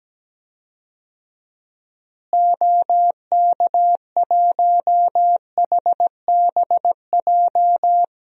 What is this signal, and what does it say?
Morse code, heard as a keyed tone, about 17 wpm, 700 Hz. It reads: OK1HBJ
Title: DXCC-100 Call Signs - 6 Characters - Repeated 3x - 17wpm